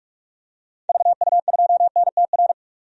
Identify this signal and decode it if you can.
Morse code, heard as a keyed tone, about 42 wpm, 700 Hz. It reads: VU2NTR
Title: DXCC-100 Call Signs - 6 Characters - 45wpm